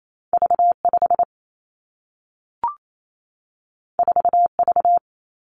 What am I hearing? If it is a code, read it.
Morse code, heard as a keyed tone, about 28 wpm, 700 Hz. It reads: V 5 4 V
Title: Two Challenging Letter-Number Combinations 28wpm